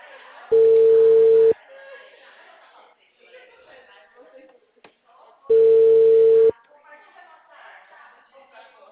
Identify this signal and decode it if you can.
Ringback tone (China, 450 Hz), with speech.